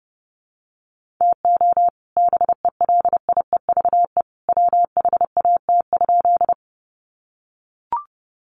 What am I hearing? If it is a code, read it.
Morse code, heard as a keyed tone, about 30 wpm, 700 Hz. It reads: TO BELIEVE WHAT?
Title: Sentences from Top 1000 Words 30wpm